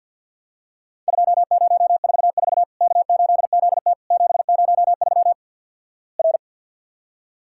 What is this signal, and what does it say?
Morse code, heard as a keyed tone, about 52 wpm, 700 Hz. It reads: 2044 K8ZT 703 R
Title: Worked All Europe DX Contest - QTC Exchange with Repeats - 50wpm